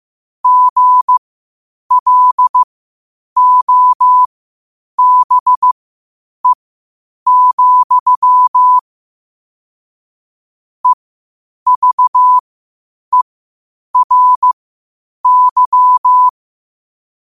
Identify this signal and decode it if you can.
Morse code, heard as a keyed tone, about 15 wpm, spaced as for 8 wpm, 1000 Hz. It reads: GLOBE, EVERY
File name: qotdfiles / 8wpm